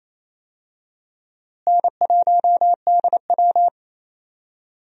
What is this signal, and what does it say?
Morse code, heard as a keyed tone, about 28 wpm, 700 Hz. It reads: N1DW
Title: USA Call Signs - 4 Characters - Repeated 2x - 28wpm